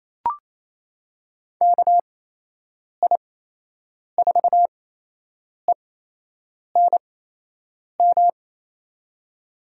Morse code, heard as a keyed tone, about 28 wpm, spaced as for 8 wpm, 700 Hz. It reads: KI4ENM
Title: USA Call Signs - 6 Characters - 8x Character Spacing - 28wpm